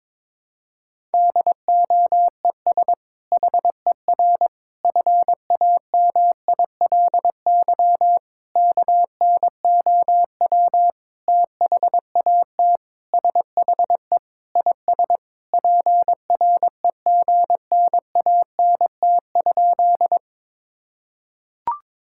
Morse code, heard as a keyed tone, about 22 wpm, 700 Hz. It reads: DOES HER FAMILY KNOW THAT SHE IS PREGNANT?
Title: Sentences from Top 1000 Words - Review Part 6 - 22wpm